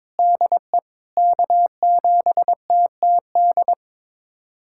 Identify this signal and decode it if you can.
Morse code, heard as a keyed tone, about 21 wpm, 700 Hz. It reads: DE K7TTD
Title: USA Callsign DE Callsign - 5 Characters - 22wpm